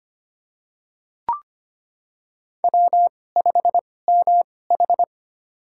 Morse code, heard as a keyed tone, about 25 wpm, spaced as for 18 wpm, 700 Hz. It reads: W5MH
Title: USA Call Signs - 4 Characters - 2x Character Spacing - 25wpm